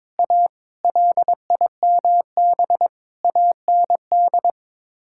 Morse code, heard as a keyed tone, about 22 wpm, 700 Hz. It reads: A LIMB AND
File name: practice / 22wpm / Sets of 8 Words - 22wpm